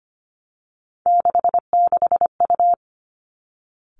Morse code, heard as a keyed tone, about 25 wpm, 700 Hz. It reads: 66U
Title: Three Challenging Letter-Number Combinations 25wpm